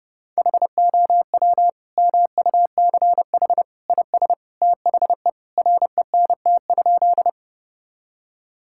Morse code, heard as a keyed tone, about 30 wpm, 700 Hz. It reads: HOW MUCH IS THE RENT?
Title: Sentences from Top 1000 Words - Review Part 5 - 30wpm